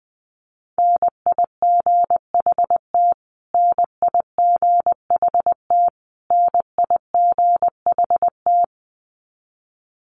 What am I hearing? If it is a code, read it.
Morse code, heard as a keyed tone, about 20 wpm, 700 Hz. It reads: NIGHT NIGHT NIGHT
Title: Top 300 Words Repeated 20wpm